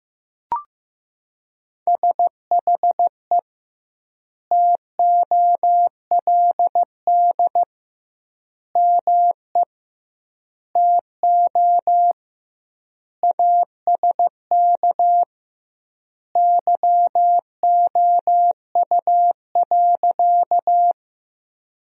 Morse code, heard as a keyed tone, about 15 wpm, 700 Hz. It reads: SHE TOLD ME TO ASK YOU.
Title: 2x Word-Spacing - Sentences from Top 400 Words - 15wpm